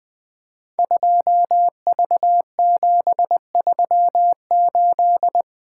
Morse code, heard as a keyed tone, about 20 wpm, 700 Hz. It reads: 2V738
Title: Code Groups - 5 Letter:Numbers - 20wpm